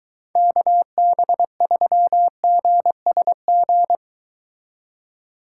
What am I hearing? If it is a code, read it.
Morse code, heard as a keyed tone, about 22 wpm, 700 Hz. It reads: KB3GSG